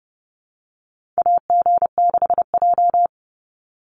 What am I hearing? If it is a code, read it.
Morse code, heard as a keyed tone, about 30 wpm, 700 Hz. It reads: AG6J
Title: USA Call Signs 30wpm